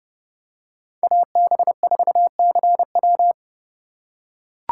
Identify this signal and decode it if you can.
Morse code, heard as a keyed tone, about 30 wpm, 700 Hz. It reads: AB4CW
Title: USA Call Signs - 5 Characters - 3x Character Spacing - 30wpm